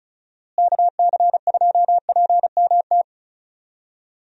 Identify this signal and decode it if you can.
Morse code, heard as a keyed tone, about 36 wpm, 700 Hz. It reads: KC2PMT
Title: USA Call Signs - 6 Characters - Repeated 2x - 35wpm